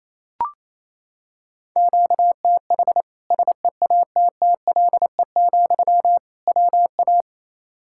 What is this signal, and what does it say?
Morse code, heard as a keyed tone, about 28 wpm, 700 Hz. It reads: QTH SEATTLE, WA
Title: Practice QSO - Exchange 1 - Beginning - 28wpm